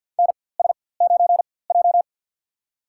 Morse code, heard as a keyed tone, about 52 wpm, spaced as for 23 wpm, 700 Hz. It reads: NS9J
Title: USA Call Signs - 4x Character Spacing - 50wpm